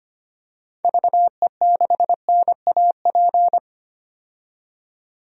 Morse code, heard as a keyed tone, about 25 wpm, 700 Hz. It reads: VE6NAP
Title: DXCC-100 Call Signs - 6 Characters - 25wpm